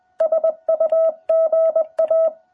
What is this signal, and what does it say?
Morse code, heard as a keyed tone, about 20 wpm, 660 Hz. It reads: SUGA
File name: jh8cbh / 2021cw